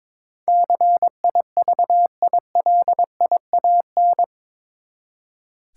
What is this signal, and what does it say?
Morse code, heard as a keyed tone, about 22 wpm, 700 Hz. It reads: CIVILIAN
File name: practice / 22wpm / Eight Letter Words - 22wpm